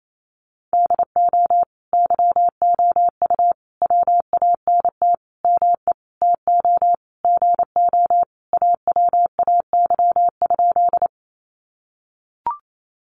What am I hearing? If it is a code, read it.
Morse code, heard as a keyed tone, about 28 wpm, 700 Hz. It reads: DO YOU WANT ME TO GO AWAY?
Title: Sentences from Top 600 Words 28wpm